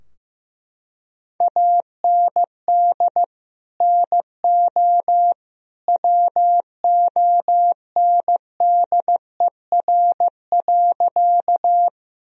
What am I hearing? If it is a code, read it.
Morse code, heard as a keyed tone, about 15 wpm, 700 Hz. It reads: AND NO WONDER.
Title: Sentences from Top 1000 Words - Review Part 3 - 15wpm